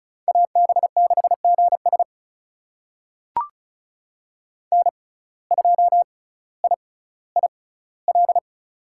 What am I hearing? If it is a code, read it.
Morse code, heard as a keyed tone, about 36 wpm, spaced as for 18 wpm, 700 Hz. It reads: AB6GS N2IIL
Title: USA Call Signs - 5 Characters - 6x Character Spacing - 35wpm